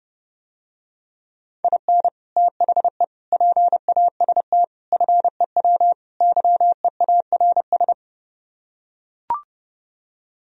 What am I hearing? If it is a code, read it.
Morse code, heard as a keyed tone, about 32 wpm, 700 Hz. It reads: IN THE PAST FEW YEARS